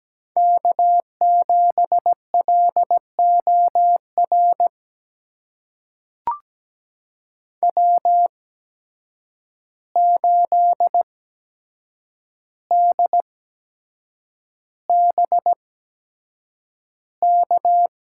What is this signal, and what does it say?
Morse code, heard as a keyed tone, about 17 wpm, spaced as for 7 wpm, 700 Hz. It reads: K7LOR W8DBK
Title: USA Call Signs - 5 Characters - 8x Character Spacing - 17wpm